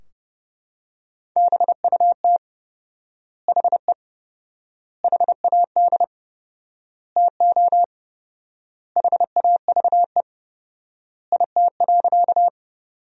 Morse code, heard as a keyed tone, about 30 wpm, 700 Hz. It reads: BUT HE HAD TO HAVE IT.
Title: 4x Word-Spacing - Sentences from Top 100 Words - 30wpm